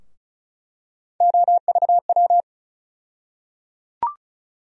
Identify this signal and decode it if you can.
Morse code, heard as a keyed tone, about 34 wpm, 700 Hz. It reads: OVW
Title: USA Call Sign Postfixes - 3 Letters 35wpm